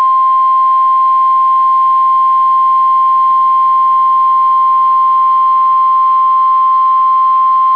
Emergency Alert System (SAME) tones (the 1050 Hz attention tone). It is heard from 0.4 to 6.9 s.